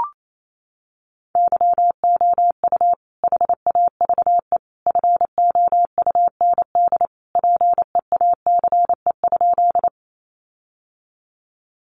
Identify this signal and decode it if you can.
Morse code, heard as a keyed tone, about 28 wpm, 700 Hz. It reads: YOU HAVE FOUND PEACE?
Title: Sentences from Top 1500 Words 28wpm